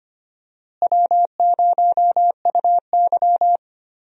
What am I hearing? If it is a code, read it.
Morse code, heard as a keyed tone, about 25 wpm, 700 Hz. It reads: W0UY
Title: USA Call Signs - 4 Characters - Repeated 3x - 25wpm